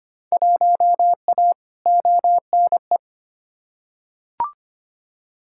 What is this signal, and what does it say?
Morse code, heard as a keyed tone, about 25 wpm, 700 Hz. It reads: 1A ONE
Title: Field Day Exchanges - 25wpm